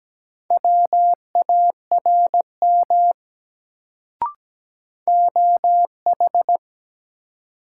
Morse code, heard as a keyed tone, about 17 wpm, 700 Hz. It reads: WARM OH
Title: Sentences from Top 500 Words - New Word Drill - 17wpm